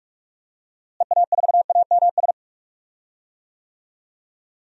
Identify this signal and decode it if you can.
Morse code, heard as a keyed tone, about 45 wpm, 700 Hz. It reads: EA4AMS
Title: DXCC-100 Call Signs - 6 Characters - Repeated 3x - 45wpm